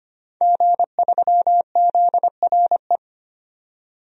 Morse code, heard as a keyed tone, about 25 wpm, 700 Hz. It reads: G3ZRE
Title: DXCC-100 Call Signs - 5 Characters - 25wpm